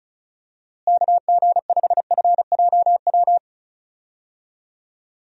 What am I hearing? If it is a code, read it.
Morse code, heard as a keyed tone, about 36 wpm, 700 Hz. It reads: KG5FJW
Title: USA Call Signs - 6 Characters - Repeated 2x - 35wpm